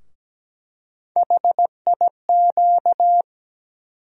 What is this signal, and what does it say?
Morse code, heard as a keyed tone, about 17 wpm, 700 Hz. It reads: HIQ
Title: USA Call Sign Postfixes - 3 Letters 17wpm